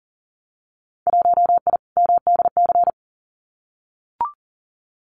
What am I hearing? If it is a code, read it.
Morse code, heard as a keyed tone, about 40 wpm, 700 Hz. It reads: 1I MDC
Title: Winter Field Day Exchanges 40wpm